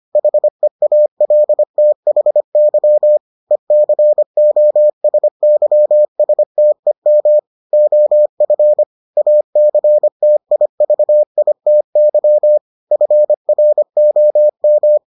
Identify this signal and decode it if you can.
Morse code, heard as a keyed tone, about 25 wpm, 600 Hz. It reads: HEALTHY ECOSYSTEM OF ACTIVITY FROM